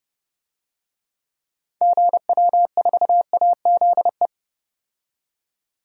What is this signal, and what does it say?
Morse code, heard as a keyed tone, about 30 wpm, 700 Hz. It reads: GW4AZE
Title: DXCC-100 Call Signs - 6 Characters - 30wpm